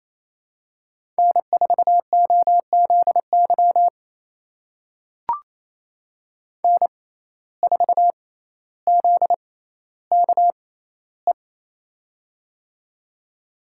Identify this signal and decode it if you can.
Morse code, heard as a keyed tone, about 28 wpm, spaced as for 14 wpm, 700 Hz. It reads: N4OZY N4ZKE
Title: USA Call Signs - 5 Characters - 6x Character Spacing - 28wpm